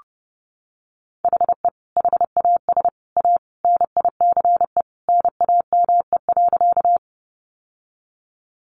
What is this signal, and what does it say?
Morse code, heard as a keyed tone, about 30 wpm, 700 Hz. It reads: HE HAS A NICE NAME.
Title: Sentences from Top 1100 Words 30wpm